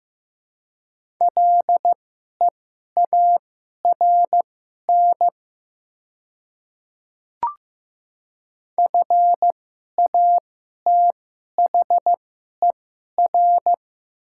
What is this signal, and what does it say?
Morse code, heard as a keyed tone, about 15 wpm, spaced as for 11 wpm, 700 Hz. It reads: LEARN FATHER
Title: 2x Character Spacing - Top 300 Words - 15wpm